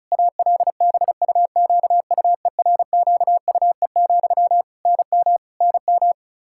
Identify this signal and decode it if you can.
Morse code, heard as a keyed tone, about 34 wpm, 700 Hz. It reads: ALBUQUERQUE, NM NM
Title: K1USN SST QSO Practice 35wpm